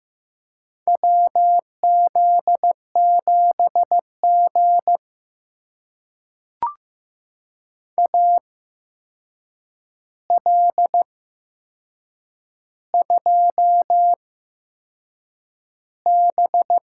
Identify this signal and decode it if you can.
Morse code, heard as a keyed tone, about 15 wpm, spaced as for 6 wpm, 700 Hz. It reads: WZ7G AL2B
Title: USA Call Signs - 4 Characters - 8x Character Spacing - 15wpm